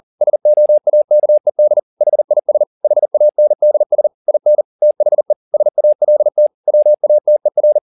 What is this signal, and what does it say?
Morse code, heard as a keyed tone, about 40 wpm, 600 Hz. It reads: SOAKED HIS HANDS IN THE SALT WATER
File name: the old man and the sea 2 40wpm